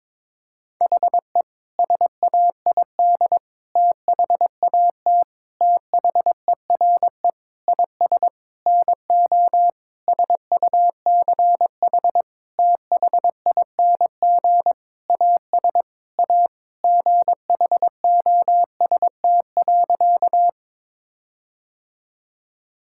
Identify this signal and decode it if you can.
Morse code, heard as a keyed tone, about 22 wpm, 700 Hz. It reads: HE SAID THAT THERE IS NO SUCH THING AS A GHOST.
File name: practice / 22wpm / Sentences with 8-Word Phrases - 22wpm